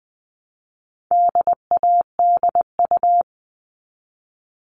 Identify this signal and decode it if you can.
Morse code, heard as a keyed tone, about 20 wpm, 700 Hz. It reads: DADU
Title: All Cut Numbers - 4-Digit 20wpm